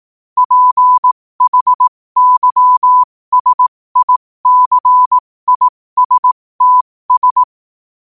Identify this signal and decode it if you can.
Morse code, heard as a keyed tone, about 18 wpm, spaced as for 15 wpm, 1000 Hz. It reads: PHYSICISTS
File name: qotdfiles / 15wpm